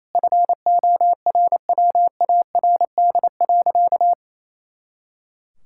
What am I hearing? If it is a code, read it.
Morse code, heard as a keyed tone, about 28 wpm, 700 Hz. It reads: FORWARD.
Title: Pangram Challenge v2 - 28wpm